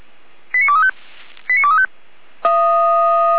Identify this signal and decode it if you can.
Five-tone selective call (ZVEI), heard as quick runs of five tones, tones of about 70 ms. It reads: 89235 89235